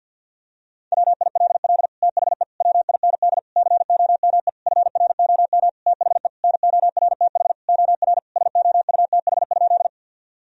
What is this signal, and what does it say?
Morse code, heard as a keyed tone, about 51 wpm, 700 Hz. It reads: WILL THE WIND COME FROM THE NORTH OR SOUTH?
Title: Sentences from Top 1000 Words - Review Part 2 - 50wpm